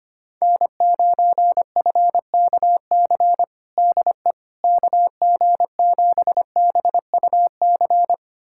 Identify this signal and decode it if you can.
Morse code, heard as a keyed tone, about 25 wpm, 700 Hz. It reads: N9FKC DE KG7BUC
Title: USA Callsign DE Callsign 25wpm